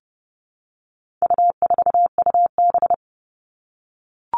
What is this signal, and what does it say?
Morse code, heard as a keyed tone, about 30 wpm, 700 Hz. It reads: U4UB
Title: All Cut Numbers - 4-Digit 30wpm